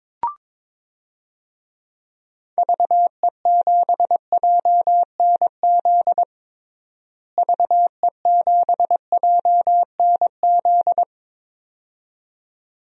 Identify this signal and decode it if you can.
Morse code, heard as a keyed tone, about 22 wpm, 700 Hz. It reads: VE7JNZ VE7JNZ
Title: DXCC-100 Call Signs - 6 Characters - Repeated 2x - 22wpm